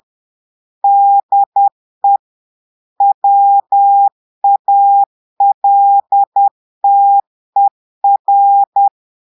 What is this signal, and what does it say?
Morse code, heard as a keyed tone, about 10 wpm, 800 Hz. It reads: DE WALTER
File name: Portugais - texte en clair 1 - vitesse - 10 mots minute